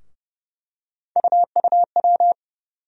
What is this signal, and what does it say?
Morse code, heard as a keyed tone, about 30 wpm, 700 Hz. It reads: UUW
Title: USA Call Sign Postfixes - 3 Letters 30wpm